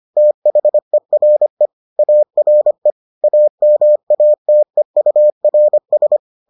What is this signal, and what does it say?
Morse code, heard as a keyed tone, about 25 wpm, 600 Hz. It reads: THERE ARE AMATEURS